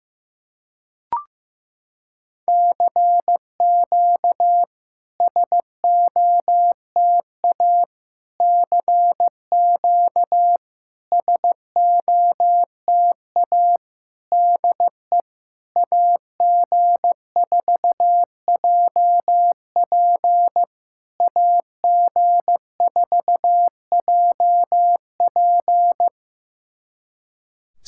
Morse code, heard as a keyed tone, about 15 wpm, 700 Hz. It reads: CQ SOTA CQ SOTA DE AG4JP AG4JP
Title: SOTA QSO Practice 15wpm